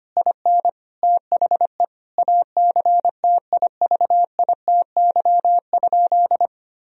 Morse code, heard as a keyed tone, about 25 wpm, 700 Hz. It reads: IN THE ACTIVITY?
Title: Sentences from Top 1000 Words - Review Part 4 - 25wpm